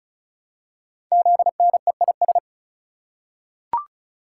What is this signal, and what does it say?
Morse code, heard as a keyed tone, about 36 wpm, 700 Hz. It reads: ZNEIS
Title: Code Groups - 5 Letters - 35wpm